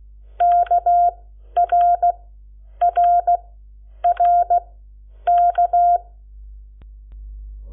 Morse code, heard as a keyed tone, about 16 wpm, spaced as for 9 wpm, 700 Hz. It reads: KRRRK
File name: Lecciones / CLASE Nº7